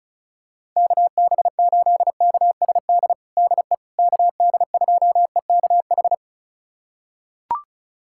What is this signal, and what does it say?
Morse code, heard as a keyed tone, about 36 wpm, 700 Hz. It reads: KB8KSD DE KD2EKH
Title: USA Callsign DE Callsign 35wpm